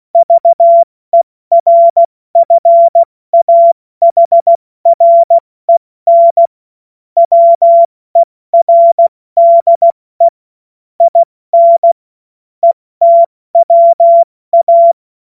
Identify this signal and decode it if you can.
Morse code, heard as a keyed tone, about 16 wpm, 680 Hz. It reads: VERFAHREN WERDE IN ETWA
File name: Morsekurs / Witze / Witz 25. Oktober 2022 70 F80